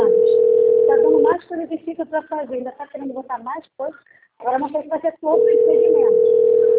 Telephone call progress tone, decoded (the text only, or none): ringback tone